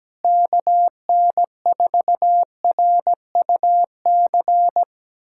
Morse code, heard as a keyed tone, about 17 wpm, 700 Hz. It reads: KN4RUC
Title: USA Callsign DE Callsign - 6 Characters - 17wpm